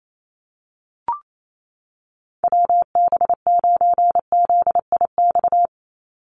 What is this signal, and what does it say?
Morse code, heard as a keyed tone, about 28 wpm, 700 Hz. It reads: WB9ZIX
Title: USA Call Signs 28wpm